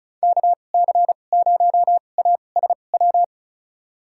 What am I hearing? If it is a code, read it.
Morse code, heard as a keyed tone, about 34 wpm, spaced as for 25 wpm, 690 Hz. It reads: KC0ASW
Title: USA Call Signs - 6 Characters - 2x Character Spacing - 35wpm